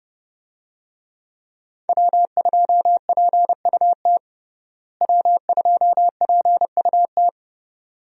Morse code, heard as a keyed tone, about 30 wpm, 700 Hz. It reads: W2PUT W2PUT
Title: USA Call Signs - 5 Characters - Repeated 2x - 30wpm